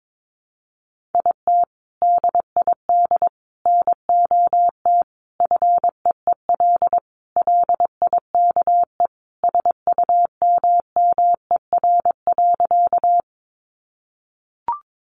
Morse code, heard as a keyed tone, about 22 wpm, 700 Hz. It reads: IT DID NOT FEEL LIKE SUMMER.
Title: Sentences from Top 1100 Words 22wpm